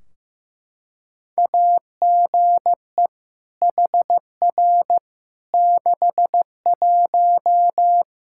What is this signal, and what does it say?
Morse code, heard as a keyed tone, about 15 wpm, 700 Hz. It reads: AGE HR 61
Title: Practice QSO - Exchange 3 - More Info - 15wpm